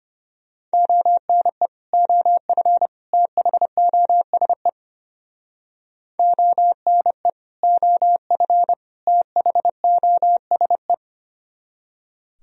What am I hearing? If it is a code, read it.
Morse code, heard as a keyed tone, about 30 wpm, 700 Hz. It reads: ONE OF THOSE ONE OF THOSE
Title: Speed-Racing - Sets of 3 Words 25wpm